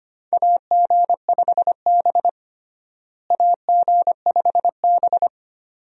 Morse code, heard as a keyed tone, about 25 wpm, 700 Hz. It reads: AG5B AG5B
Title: USA Call Signs - 4 Characters - Repeated 2x - 25wpm